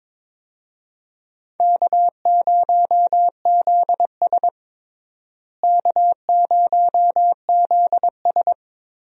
Morse code, heard as a keyed tone, about 22 wpm, 700 Hz. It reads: K0ZS K0ZS
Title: USA Call Signs - 4 Characters - Repeated 2x - 22wpm